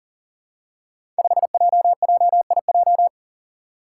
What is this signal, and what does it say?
Morse code, heard as a keyed tone, about 40 wpm, 700 Hz. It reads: 5JJIJ
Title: Five Challenging Letter-Number Combinations 40wpm